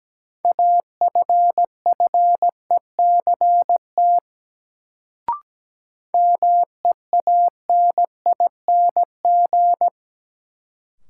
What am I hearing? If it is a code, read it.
Morse code, heard as a keyed tone, about 17 wpm, 700 Hz. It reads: AFFECT MEANING
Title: Sentences from Top 900 Words - New Word Drill - 17wpm